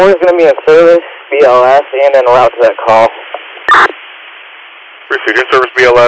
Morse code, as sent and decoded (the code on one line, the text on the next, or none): none